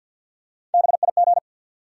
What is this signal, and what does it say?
Morse code, heard as a keyed tone, about 47 wpm, 700 Hz. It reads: BIG
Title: Sets of 2 Words - New Word Drill - 50wpm